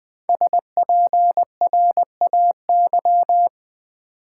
Morse code, heard as a keyed tone, about 21 wpm, 700 Hz. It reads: SPRAY
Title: Five Letter Words - 20wpm